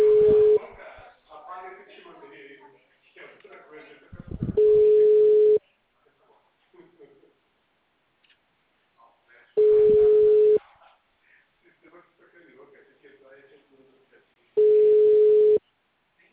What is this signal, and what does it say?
Ringback tone (Europe, 425 Hz), with speech.